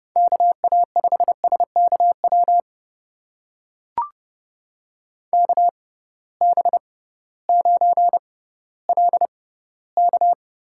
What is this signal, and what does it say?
Morse code, heard as a keyed tone, about 31 wpm, spaced as for 16 wpm, 700 Hz. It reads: KA5SKW KB9LK
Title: USA Call Signs - 6 Characters - 6x Character Spacing - 30wpm